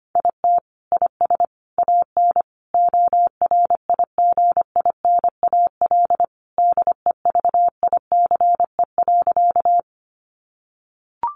Morse code, heard as a keyed tone, about 25 wpm, 700 Hz. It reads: IT IS AN ORIGINAL DEVICE.
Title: Sentences from Top 1500 Words 25wpm